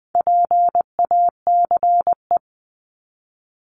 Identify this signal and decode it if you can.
Morse code, heard as a keyed tone, about 20 wpm, 700 Hz. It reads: PACE
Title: 260 Lesson 42e - Words 20-20wpm